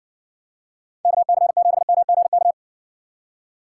Morse code, heard as a keyed tone, about 61 wpm, 700 Hz. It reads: KC7MGK